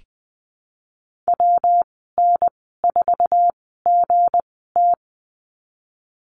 Morse code, heard as a keyed tone, about 20 wpm, spaced as for 15 wpm, 700 Hz. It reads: WN4GT
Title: 219 Lesson 35g - Number 0 - Call Signs 20wpm